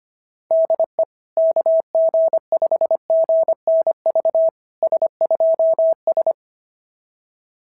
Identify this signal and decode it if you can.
Morse code, heard as a keyed tone, about 22 wpm, 650 Hz. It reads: DE KG5GNV S2S
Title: SOTA QSO Practice 25wpm